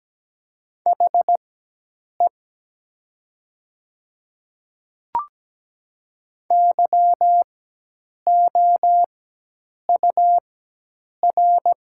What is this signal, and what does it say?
Morse code, heard as a keyed tone, about 17 wpm, spaced as for 8 wpm, 700 Hz. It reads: HE YOUR
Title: 4x Character Spacing - Top 100 Words - 17wpm